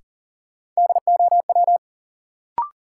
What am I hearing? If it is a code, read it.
Morse code, heard as a keyed tone, about 39 wpm, 700 Hz. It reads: DOW
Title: Sets of 5 Words - New Word Drill - 40wpm